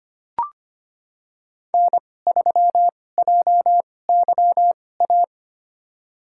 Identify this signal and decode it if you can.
Morse code, heard as a keyed tone, about 25 wpm, spaced as for 18 wpm, 700 Hz. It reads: N3JYA
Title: USA Call Signs - 5 Characters - 2x Character Spacing - 25wpm